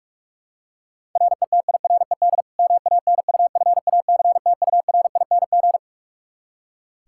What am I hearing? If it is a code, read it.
Morse code, heard as a keyed tone, about 41 wpm, 700 Hz. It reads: RETIRED MANUFACTURING
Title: K1USN SST QSO Practice 45wpm